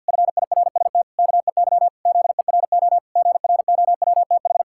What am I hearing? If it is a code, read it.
Morse code, heard as a keyed tone, about 50 wpm, 700 Hz. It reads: FIRST KEY ZERO GROWTH
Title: Pangram Challenge v2 - 50wpm